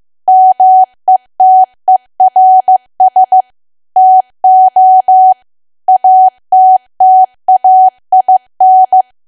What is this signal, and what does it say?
Morse code, heard as a keyed tone, about 15 wpm, 750 Hz. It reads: METERS TO ATTAIN
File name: cwa / W1AW-Level2 / 15WPM